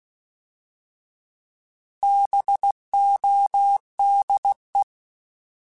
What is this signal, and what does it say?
Morse code, heard as a keyed tone, about 16 wpm, 800 Hz. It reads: BODE
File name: Alphabetical / Alphabetical mp3s / Bs @15